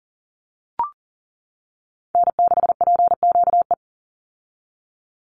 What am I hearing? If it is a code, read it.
Morse code, heard as a keyed tone, about 40 wpm, 700 Hz. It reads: N6PQE
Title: USA Call Signs 40wpm